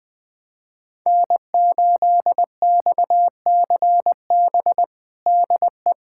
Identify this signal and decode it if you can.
Morse code, heard as a keyed tone, about 21 wpm, 700 Hz. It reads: N8XCB DE
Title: USA Callsign DE Callsign - 5 Characters - 20wpm